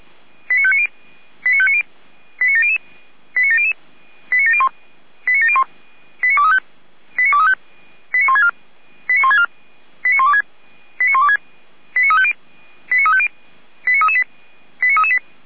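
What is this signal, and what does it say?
Five-tone selective call (ZVEI), heard as quick runs of five tones, tones of about 70 ms. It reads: 89590 89590 89700 89700 89701 89701 89235 89164 89164 89126 89126 89350 89350 89338 89338
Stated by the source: ZVEI five-tone selective call received by SDR.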